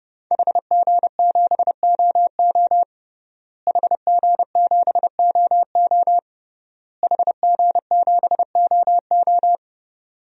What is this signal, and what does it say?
Morse code, heard as a keyed tone, about 30 wpm, 700 Hz. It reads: HG7OO HG7OO HG7OO
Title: DXCC-100 Call Signs - 5 Characters - Repeated 3x - 30wpm